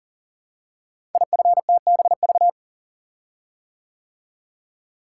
Morse code, heard as a keyed tone, about 41 wpm, 700 Hz. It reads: IFTBV
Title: Code Groups - 5 Letters - 40wpm